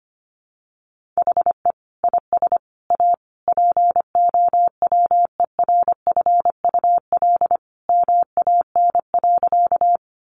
Morse code, heard as a keyed tone, about 25 wpm, 700 Hz. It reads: HE IS A POWERFUL MAN.
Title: Sentences from Top 1300 Words 25wpm